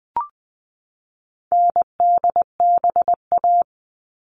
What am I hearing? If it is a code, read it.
Morse code, heard as a keyed tone, about 20 wpm, 700 Hz. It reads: NDBA
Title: All Cut Numbers - 4-Digit 20wpm